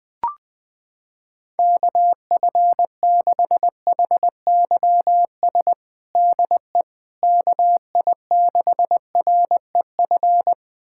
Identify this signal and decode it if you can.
Morse code, heard as a keyed tone, about 21 wpm, 700 Hz. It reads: KF6HYS DE KI6REF
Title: USA Callsign DE Callsign - 6 Characters - 20wpm